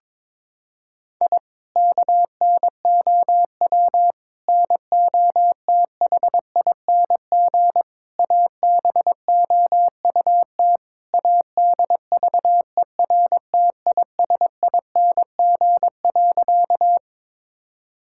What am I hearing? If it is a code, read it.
Morse code, heard as a keyed tone, about 22 wpm, 700 Hz. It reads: I KNOW NOTHING ABOUT ADVERTISING.
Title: Sentences from Top 1000 Words - Review Part 6 - 22wpm